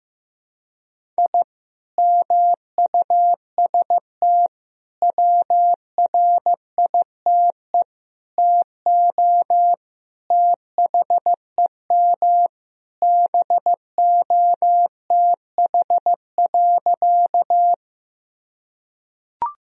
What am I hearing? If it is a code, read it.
Morse code, heard as a keyed tone, about 15 wpm, 700 Hz. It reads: I MUST WRITE TO THEM BOTH.
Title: Sentences from Top 1000 Words - Review Part 1 - 15wpm